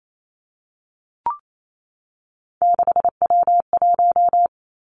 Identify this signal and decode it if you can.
Morse code, heard as a keyed tone, about 28 wpm, 700 Hz. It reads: 6W1
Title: Three Challenging Letter-Number Combinations 28wpm